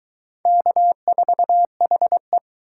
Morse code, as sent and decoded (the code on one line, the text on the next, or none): -.- ....- .... .
K4HE